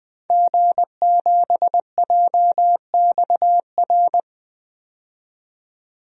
Morse code, heard as a keyed tone, about 20 wpm, 700 Hz. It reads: G7JXR
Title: DXCC-100 Call Signs - 5 Characters - 20wpm